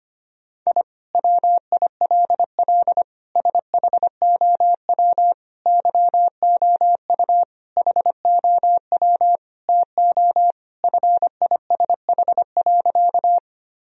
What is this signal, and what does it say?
Morse code, heard as a keyed tone, about 25 wpm, 700 Hz. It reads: I WILL SHOW YOU HOW TO FISH.
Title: Sentences from Top 1000 Words - Review Part 1 - 25wpm